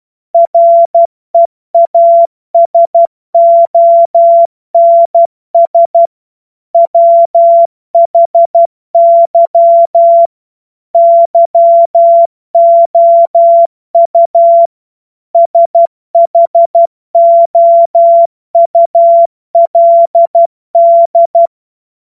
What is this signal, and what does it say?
Morse code, heard as a keyed tone, about 12 wpm, 680 Hz. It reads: REASONS WHY YOU SHOULD
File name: Morsekurs / Witze